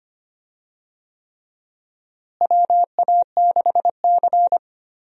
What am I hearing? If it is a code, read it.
Morse code, heard as a keyed tone, about 25 wpm, 700 Hz. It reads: WA6C
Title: USA Call Signs - 4 Characters - Repeated 3x - 25wpm